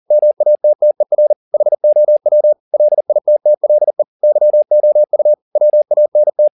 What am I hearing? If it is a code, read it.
Morse code, heard as a keyed tone, about 40 wpm, 600 Hz. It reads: MATTER HOW LITTLE YOU WANT